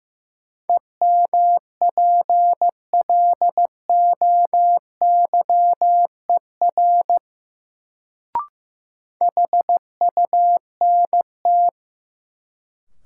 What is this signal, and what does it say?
Morse code, heard as a keyed tone, about 15 wpm, 700 Hz. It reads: EMPLOYER HUNT
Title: Sentences from Top 1600 Words - New Word Drill - 15wpm